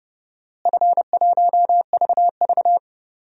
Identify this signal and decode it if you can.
Morse code, heard as a keyed tone, about 30 wpm, 700 Hz. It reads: F1VV
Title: DXCC-100 Call Signs - 4 Characters - Repeated 2x - 30wpm